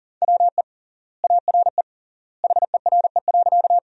Morse code, heard as a keyed tone, about 40 wpm, 700 Hz. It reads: WE ARE HERE.
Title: 3x Word-Spacing - Sentences from Top 200 Words - 40wpm